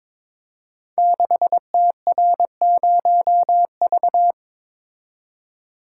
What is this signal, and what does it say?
Morse code, heard as a keyed tone, about 22 wpm, 700 Hz. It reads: 6TR0V
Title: Code Groups - 5 Letter:Numbers - 22wpm